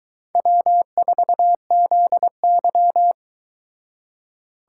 Morse code, heard as a keyed tone, about 23 wpm, 700 Hz. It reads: W4ZY